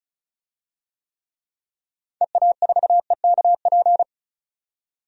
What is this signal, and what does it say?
Morse code, heard as a keyed tone, about 34 wpm, 700 Hz. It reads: EA4EKP
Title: DXCC-100 Call Signs - 6 Characters - 35wpm